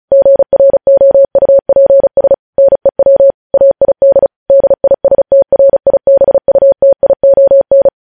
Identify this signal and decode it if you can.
Morse code, heard as a keyed tone, about 36 wpm, 550 Hz. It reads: GROUPS NEW AID DISTRIBUTION
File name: podcast / News headlines-1x-35x35